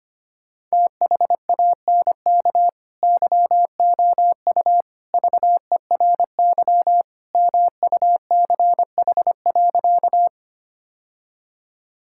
Morse code, heard as a keyed tone, about 25 wpm, 700 Hz. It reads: THANK YOU VERY MUCH.
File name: practice / 25wpm / Sentences with 3-Word Phrases - 25wpm